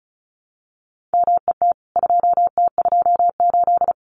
Morse code, heard as a keyed tone, about 34 wpm, 700 Hz. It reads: MET 2T28
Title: CWT Exchange 35wpm